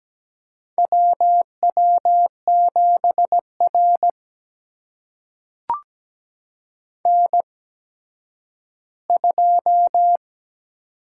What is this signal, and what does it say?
Morse code, heard as a keyed tone, about 17 wpm, spaced as for 10 wpm, 700 Hz. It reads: WW7R N2
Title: USA Call Signs - 4 Characters - 8x Character Spacing - 17wpm